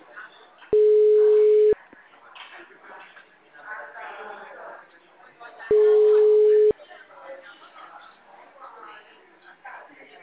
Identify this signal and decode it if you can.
Ringback tone (Europe, 425 Hz), with speech.
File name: out- -2016- - - .29002